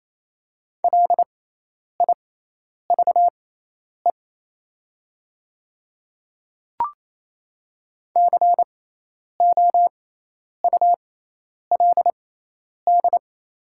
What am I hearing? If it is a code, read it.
Morse code, heard as a keyed tone, about 28 wpm, spaced as for 10 wpm, 700 Hz. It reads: LIVE COULD
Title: 6x Character Spacing - Top 200 Words - 28wpm